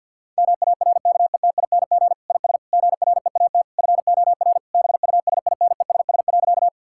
Morse code, heard as a keyed tone, about 50 wpm, 700 Hz. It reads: MARKETING IS GREAT FOR BUSINESS.
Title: Sentences from Top 1000 Words - Review Part 5 - 50wpm